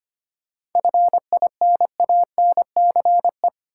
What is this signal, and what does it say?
Morse code, heard as a keyed tone, about 25 wpm, 700 Hz. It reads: FINANCE
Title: Seven Letter Words - 25wpm